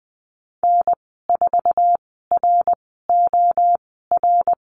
Morse code, heard as a keyed tone, about 20 wpm, spaced as for 14 wpm, 700 Hz. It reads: N4ROR
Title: 055 Lesson 10g - Letter r - Call Signs 20wpm